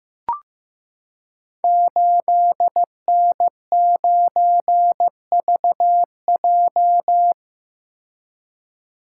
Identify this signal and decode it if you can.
Morse code, heard as a keyed tone, about 15 wpm, 700 Hz. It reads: 8N9VJ
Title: Code Groups - 5 Letter:Numbers - 15wpm